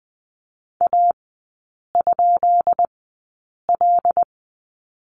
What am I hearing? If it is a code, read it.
Morse code, heard as a keyed tone, about 20 wpm, spaced as for 8 wpm, 700 Hz. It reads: A?L
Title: 152 Lesson 25e - Letter p - Three Characters 20wpm